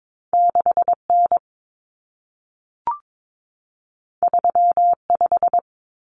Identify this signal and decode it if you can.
Morse code, heard as a keyed tone, about 22 wpm, 700 Hz. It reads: 6N 35
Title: Common Cut Numbers - 2-Digit 22wpm